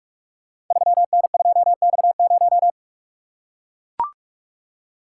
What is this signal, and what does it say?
Morse code, heard as a keyed tone, about 44 wpm, 700 Hz. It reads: 3N2X0
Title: Code Groups - 5 Letter:Numbers - 45wpm